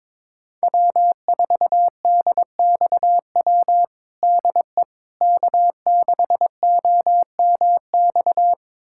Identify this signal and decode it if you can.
Morse code, heard as a keyed tone, about 21 wpm, 700 Hz. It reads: W4DXW DE K6OMX
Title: USA Callsign DE Callsign - 5 Characters - 22wpm